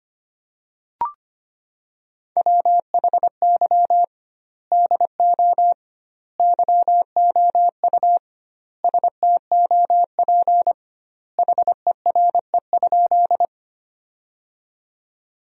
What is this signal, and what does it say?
Morse code, heard as a keyed tone, about 25 wpm, 700 Hz. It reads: WHY DO YOU STOP HERE?
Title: 2x Word-Spacing - Sentences from Top 300 Words - 25wpm